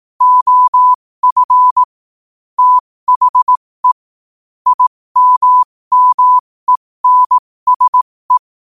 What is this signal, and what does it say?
Morse code, heard as a keyed tone, about 18 wpm, spaced as for 15 wpm, 1000 Hz. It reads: OF THE IMMENSE
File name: qotdfiles / 15wpm